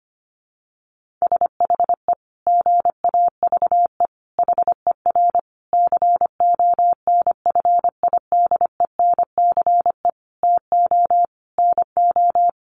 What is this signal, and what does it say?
Morse code, heard as a keyed tone, about 25 wpm, 700 Hz. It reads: SHE GAVE HER CONFIDENCE TO NO
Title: Sentences from Top 1500 Words 25wpm